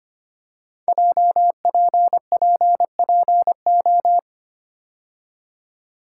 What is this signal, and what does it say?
Morse code, heard as a keyed tone, about 25 wpm, 700 Hz. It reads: JPPPO
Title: Code Groups - 5 Letters - 25wpm